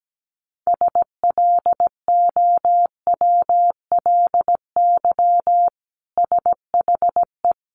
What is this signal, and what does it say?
Morse code, heard as a keyed tone, about 17 wpm, 700 Hz. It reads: SLOWLY SHE
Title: Sentences from Top 1100 Words 17wpm